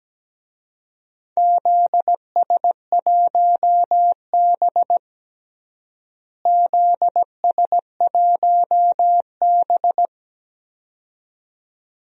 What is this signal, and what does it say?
Morse code, heard as a keyed tone, about 17 wpm, 700 Hz. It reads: ZS1B ZS1B
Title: DXCC-100 Call Signs - 4 Characters - Repeated 2x - 17wpm